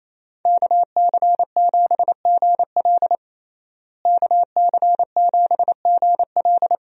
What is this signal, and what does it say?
Morse code, heard as a keyed tone, about 28 wpm, 700 Hz. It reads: KC7GL KC7GL
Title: USA Call Signs - 5 Characters - Repeated 2x - 28wpm